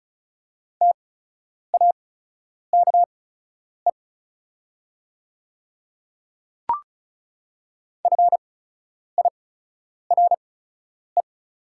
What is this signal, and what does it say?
Morse code, heard as a keyed tone, about 36 wpm, spaced as for 10 wpm, 690 Hz. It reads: TAKE FIRE
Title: 8x Character Spacing - Top 500 Words - 35wpm